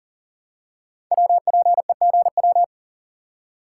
Morse code, heard as a keyed tone, about 41 wpm, 700 Hz. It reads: WPEGW
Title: Code Groups - 5 Letter:Numbers - 40wpm